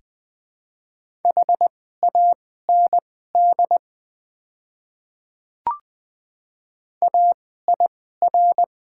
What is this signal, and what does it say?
Morse code, heard as a keyed tone, about 20 wpm, spaced as for 14 wpm, 700 Hz. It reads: HAND AIR
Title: 2x Character Spacing - Top 200 Words - 20wpm